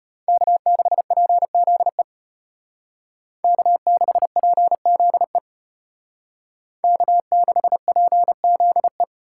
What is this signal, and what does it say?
Morse code, heard as a keyed tone, about 32 wpm, 700 Hz. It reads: K6PZE K6PZE K6PZE